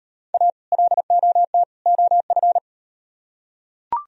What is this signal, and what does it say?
Morse code, heard as a keyed tone, about 38 wpm, 700 Hz. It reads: A LOT OF